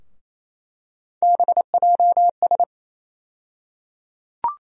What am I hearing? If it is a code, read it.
Morse code, heard as a keyed tone, about 28 wpm, 700 Hz. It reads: BJS
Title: USA Call Sign Postfixes - 3 Letters 28wpm